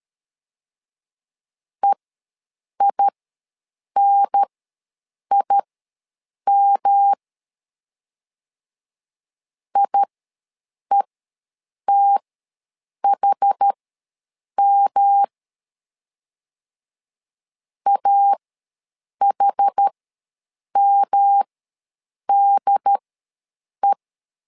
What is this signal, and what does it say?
Morse code, heard as a keyed tone, about 13 wpm, spaced as for 7 wpm, 800 Hz. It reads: EINIM IETHM AHMDE
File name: LEZIONI / 3 MWRO / 3 M ws 12wpm 1745